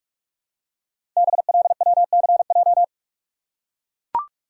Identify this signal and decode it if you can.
Morse code, heard as a keyed tone, about 44 wpm, 700 Hz. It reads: BLWCJ
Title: Code Groups - 5 Letters - 45wpm